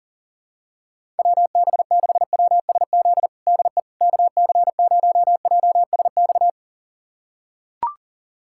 Morse code, heard as a keyed tone, about 37 wpm, 700 Hz. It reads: WB6WSZ DE KC0JSX
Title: USA Callsign DE Callsign 40wpm